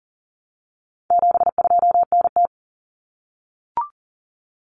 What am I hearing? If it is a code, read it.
Morse code, heard as a keyed tone, about 40 wpm, 700 Hz. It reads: 72NT
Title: Common Cut Numbers - 4-Digit 40wpm